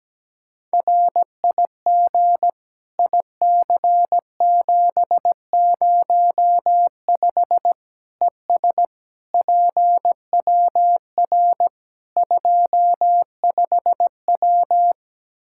Morse code, heard as a keyed tone, about 17 wpm, 700 Hz. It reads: RIG IC705 ES PWR 25W
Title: Practice QSO - Exchange 2 - Info - 17wpm